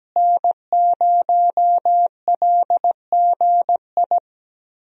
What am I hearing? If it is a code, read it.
Morse code, heard as a keyed tone, about 17 wpm, 700 Hz. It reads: N0LGI
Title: USA Call Signs - 5 Characters - Repeated 2x - 17wpm